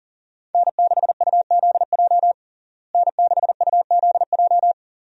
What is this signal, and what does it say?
Morse code, heard as a keyed tone, about 40 wpm, 700 Hz. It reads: N6UZJ N6UZJ
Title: USA Call Signs - 5 Characters - Repeated 3x - 40wpm